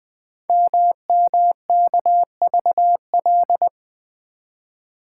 Morse code, heard as a keyed tone, about 20 wpm, 700 Hz. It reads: MMKVL
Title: Code Groups - 5 Letters - 20wpm